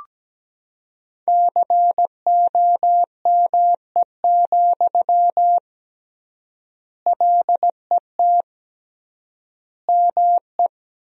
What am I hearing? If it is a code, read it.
Morse code, heard as a keyed tone, about 17 wpm, 700 Hz. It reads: COME, LET ME
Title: 3x Word-Spacing - Sentences from Top 500 Words - 17wpm